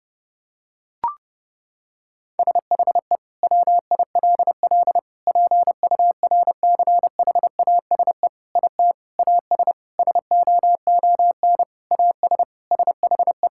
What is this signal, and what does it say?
Morse code, heard as a keyed tone, about 32 wpm, 700 Hz. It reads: SHE WILL PURCHASE IT AS SOON AS SHE
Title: Sentences from Top 1000 Words - Review Part 5 - 30wpm